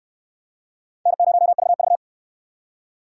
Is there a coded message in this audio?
Yes: Morse code from 1.1 to 2.0 s, heard as a keyed tone, about 63 wpm, 700 Hz. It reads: N0VV